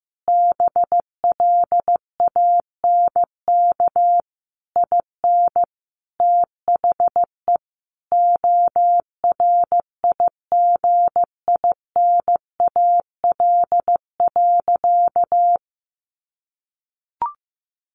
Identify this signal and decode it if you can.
Morse code, heard as a keyed tone, about 15 wpm, 700 Hz. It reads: BLANK IN THE ORIGINAL.
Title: Sentences from Top 1500 Words 15wpm